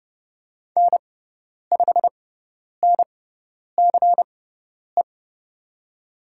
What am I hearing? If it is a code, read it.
Morse code, heard as a keyed tone, about 30 wpm, spaced as for 10 wpm, 700 Hz. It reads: N5NCE
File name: Lesson 16g - Letter c - Call Signs 30wpm